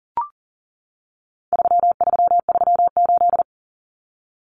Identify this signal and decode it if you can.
Morse code, heard as a keyed tone, about 40 wpm, 700 Hz. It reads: 3338
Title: Common Cut Numbers - 4-Digit 40wpm